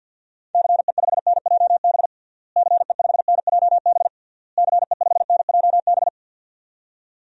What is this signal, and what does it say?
Morse code, heard as a keyed tone, about 52 wpm, 700 Hz. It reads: CE5NJB CE5NJB CE5NJB
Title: DXCC-100 Call Signs - 6 Characters - Repeated 3x - 50wpm